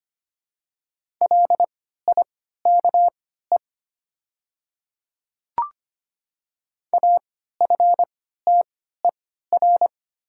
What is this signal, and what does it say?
Morse code, heard as a keyed tone, about 25 wpm, spaced as for 14 wpm, 690 Hz. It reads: LIKE AFTER
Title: 3x Character Spacing - Top 300 Words - 25wpm